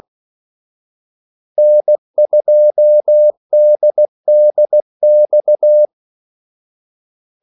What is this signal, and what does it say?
Morse code, heard as a keyed tone, about 16 wpm, 600 Hz. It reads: N2DDX
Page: Kaldesignaler 16 wpm